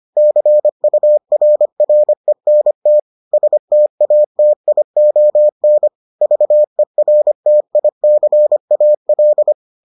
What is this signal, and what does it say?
Morse code, heard as a keyed tone, about 25 wpm, 600 Hz. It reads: CURRENT STATION VERTICAL